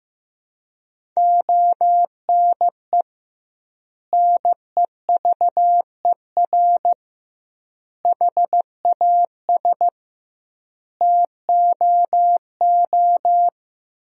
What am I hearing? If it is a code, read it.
Morse code, heard as a keyed tone, about 15 wpm, 700 Hz. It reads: ONE NEVER HAS TOO
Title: 2x Word-Spacing - Sentences from Top 300 Words - 15wpm